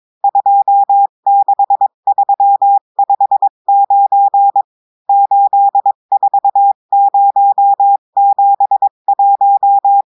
Morse code, heard as a keyed tone, about 22 wpm, 800 Hz. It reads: 26359 84071
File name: cours-du-mercredi / audios / cours-853-du-2-mai-2024-etxte chiffre et texte clair-a-20-wpm